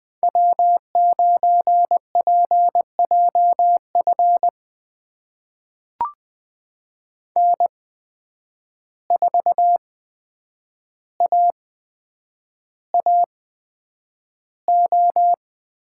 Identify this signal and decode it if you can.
Morse code, heard as a keyed tone, about 20 wpm, spaced as for 9 wpm, 700 Hz. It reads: W9PJF N4AAO
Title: USA Call Signs - 5 Characters - 8x Character Spacing - 20wpm